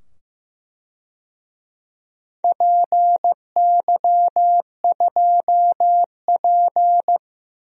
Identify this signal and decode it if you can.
Morse code, heard as a keyed tone, about 16 wpm, 700 Hz. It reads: PY2P
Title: DXCC-100 Call Signs - 4 Characters - 15wpm